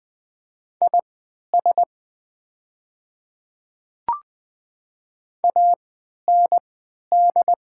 Morse code, heard as a keyed tone, about 20 wpm, spaced as for 11 wpm, 700 Hz. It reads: IS AND
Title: 3x Character Spacing - Top 100 Words - 20wpm